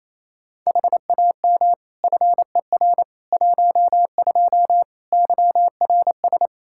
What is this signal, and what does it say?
Morse code, heard as a keyed tone, about 28 wpm, 700 Hz. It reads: HAM FER 12 YRS
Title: Practice QSO - Exchange 3 - More Info - 28wpm